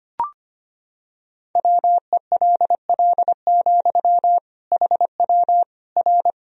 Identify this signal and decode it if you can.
Morse code, heard as a keyed tone, about 25 wpm, 700 Hz. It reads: WELL, HW R
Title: Sentences from Top 200 Words and Abbreviations 25wpm